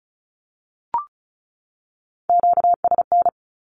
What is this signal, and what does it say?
Morse code, heard as a keyed tone, about 34 wpm, 700 Hz. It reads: QSN
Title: Q-Codes 35wpm